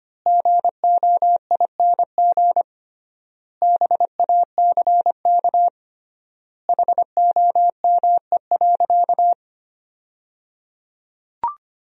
Morse code, heard as a keyed tone, about 25 wpm, 700 Hz. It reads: GOING BACK HOME.
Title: 3x Word-Spacing - Sentences from Top 500 Words - 25wpm